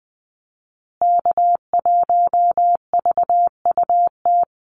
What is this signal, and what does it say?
Morse code, heard as a keyed tone, about 20 wpm, 700 Hz. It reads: K1VUT
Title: Call Signs Repeated 2x - 20wpm